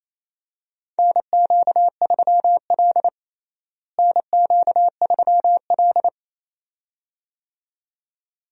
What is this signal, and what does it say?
Morse code, heard as a keyed tone, about 28 wpm, 700 Hz. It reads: NQ3L NQ3L
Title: USA Call Signs - 4 Characters - Repeated 2x - 28wpm